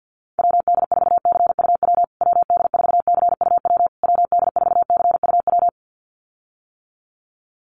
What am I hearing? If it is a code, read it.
Morse code, heard as a keyed tone, about 48 wpm, 700 Hz. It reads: WD4CUW WD4CUW WD4CUW
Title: USA Call Signs Repeated 3x 50wpm